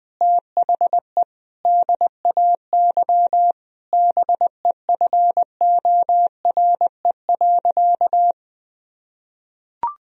Morse code, heard as a keyed tone, about 20 wpm, 700 Hz. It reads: THE DAY BEFORE.
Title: Sentences from Top 1000 Words - Review Part 1 - 20wpm